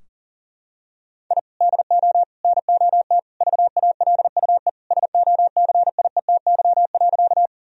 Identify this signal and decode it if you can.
Morse code, heard as a keyed tone, about 40 wpm, 700 Hz. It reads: I DO NOT VALUE SOCIETY.
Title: Sentences from Top 1000 Words - Review Part 4 - 40wpm